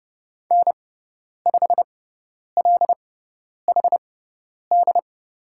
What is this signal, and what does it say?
Morse code, heard as a keyed tone, about 30 wpm, spaced as for 10 wpm, 690 Hz. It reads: N5LHD
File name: Lesson 18g - Letter m - Call Signs 30wpm